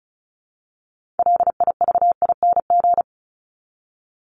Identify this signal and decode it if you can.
Morse code, heard as a keyed tone, about 34 wpm, 700 Hz. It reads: LIVING
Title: ING Suffix - Top 100 Words with Base Words 35wpm